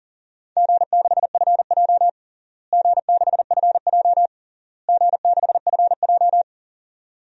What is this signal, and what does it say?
Morse code, heard as a keyed tone, about 40 wpm, 700 Hz. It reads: G6FJ G6FJ G6FJ
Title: DXCC-100 Call Signs - 4 Characters - Repeated 3x - 40wpm